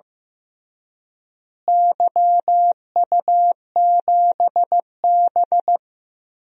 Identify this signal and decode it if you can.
Morse code, heard as a keyed tone, about 15 wpm, 700 Hz. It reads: YU7B
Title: DXCC-100 Call Signs - 4 Characters - Repeated 2x - 15wpm